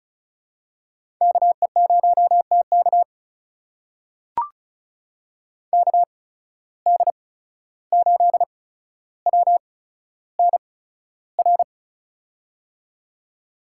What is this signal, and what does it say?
Morse code, heard as a keyed tone, about 34 wpm, spaced as for 14 wpm, 700 Hz. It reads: KE0TK KD8WNR
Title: USA Call Signs - 8x Character Spacing - 35wpm